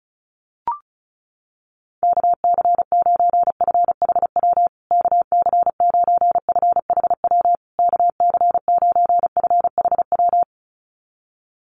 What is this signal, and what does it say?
Morse code, heard as a keyed tone, about 34 wpm, 700 Hz. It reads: KC9FHW KC9FHW KC9FHW
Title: USA Call Signs Repeated 3x 35wpm